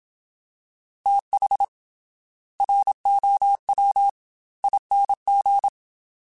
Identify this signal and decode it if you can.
Morse code, heard as a keyed tone, about 27 wpm, 800 Hz. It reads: TH ROW ING
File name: TH / TH mp3s / Fast TH @25